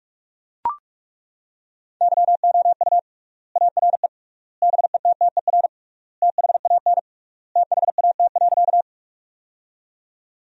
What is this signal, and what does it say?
Morse code, heard as a keyed tone, about 42 wpm, 700 Hz. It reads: YOU ARE BETTER THAN THAT.
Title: 3x Word-Spacing - Sentences from Top 400 Words - 45wpm